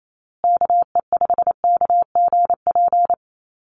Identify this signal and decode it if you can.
Morse code, heard as a keyed tone, about 28 wpm, 700 Hz. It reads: KE5KGP
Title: USA Call Signs Repeated 3x 28wpm